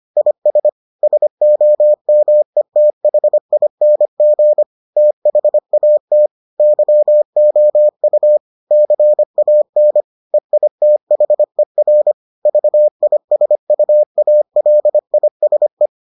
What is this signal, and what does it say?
Morse code, heard as a keyed tone, about 25 wpm, 600 Hz. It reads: IS SOMETHING THAT YOU CAN EITHER VISUALISE